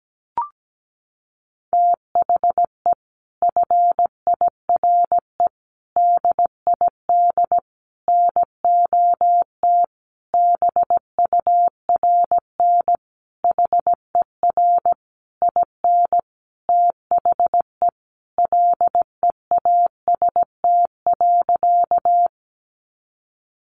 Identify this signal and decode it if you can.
Morse code, heard as a keyed tone, about 17 wpm, 700 Hz. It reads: THE FIRE DID NOT BURN HER IN THE LEAST.
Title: Sentences from Top 1300 Words 17wpm